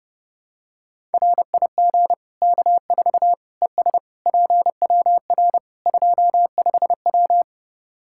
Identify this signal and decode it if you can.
Morse code, heard as a keyed tone, about 32 wpm, 700 Hz. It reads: RIG K4 ES PWR 25W
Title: Practice QSO - Exchange 2 - Info - 30wpm